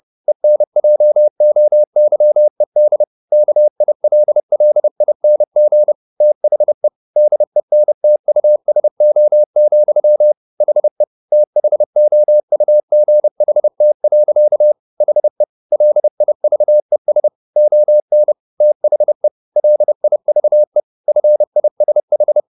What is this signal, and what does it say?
Morse code, heard as a keyed tone, about 30 wpm, 600 Hz. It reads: ENJOYED KILLING THE DENTUSO, HE THOUGHT. HE LIVES ON THE LIVE FISH